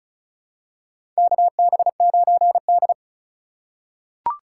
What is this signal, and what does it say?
Morse code, heard as a keyed tone, about 36 wpm, 700 Hz. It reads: KB9D
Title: USA Call Signs - 4 Characters - 2x Character Spacing - 35wpm